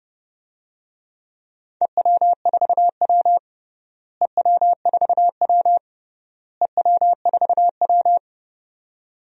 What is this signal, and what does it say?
Morse code, heard as a keyed tone, about 29 wpm, 700 Hz. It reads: EW4W EW4W EW4W
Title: DXCC-100 Call Signs - 4 Characters - Repeated 3x - 30wpm